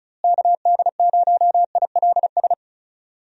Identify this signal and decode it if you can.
Morse code, heard as a keyed tone, about 36 wpm, 700 Hz. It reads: KD0ILS
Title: USA Call Signs - 6 Characters - Repeated 3x - 35wpm